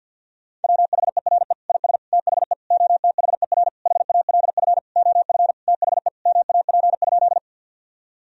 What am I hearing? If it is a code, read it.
Morse code, heard as a keyed tone, about 50 wpm, 700 Hz. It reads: WHERE IS THE OTHER HALF OF THE MAP?
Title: Sentences from Top 1000 Words - Review Part 2 - 50wpm